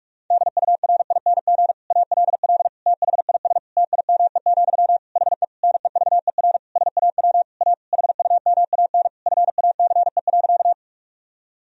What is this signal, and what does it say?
Morse code, heard as a keyed tone, about 46 wpm, 700 Hz. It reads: DURING ALL THIS TIME, HE NEVER SAW A HUMAN FACE.
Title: Sentences from Top 1000 Words - Review Part 3 - 45wpm